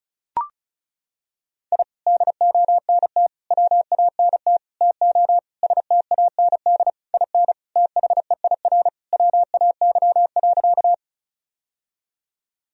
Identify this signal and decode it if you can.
Morse code, heard as a keyed tone, about 36 wpm, 700 Hz. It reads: I DONT WANT TO STAND IN THEIR WAY.
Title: Sentences from Top 1000 Words - Review Part 1 - 35wpm